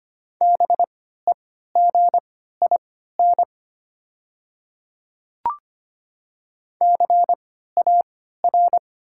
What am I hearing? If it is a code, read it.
Morse code, heard as a keyed tone, about 25 wpm, spaced as for 14 wpm, 700 Hz. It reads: BEGIN CAR
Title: 3x Character Spacing - Top 300 Words - 25wpm